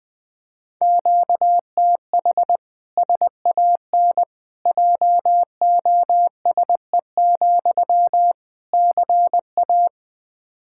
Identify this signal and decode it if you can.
Morse code, heard as a keyed tone, about 20 wpm, 700 Hz. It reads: QTH SAN JOSE, CA
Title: Practice QSO - Exchange 1 - Beginning - 20wpm